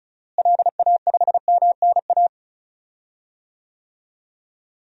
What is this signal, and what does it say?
Morse code, heard as a keyed tone, about 36 wpm, 700 Hz. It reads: LA5MNA
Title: DXCC-100 Call Signs - 6 Characters - Repeated 2x - 35wpm